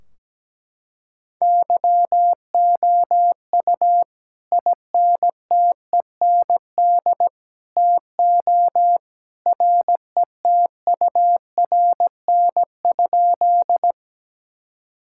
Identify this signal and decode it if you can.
Morse code, heard as a keyed tone, about 17 wpm, 700 Hz. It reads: YOU INTEND TO RETURN?
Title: Sentences from Top 1600 Words 17wpm